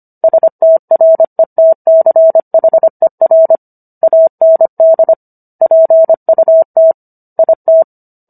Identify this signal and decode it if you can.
Morse code, heard as a keyed tone, about 25 wpm, 650 Hz. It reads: STRETCHER AND PUT IT